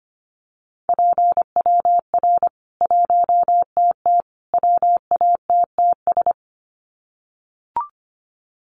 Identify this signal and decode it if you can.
Morse code, heard as a keyed tone, about 25 wpm, 700 Hz. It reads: PWR 1TT WATTS
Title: Common QSO Phrases 25wpm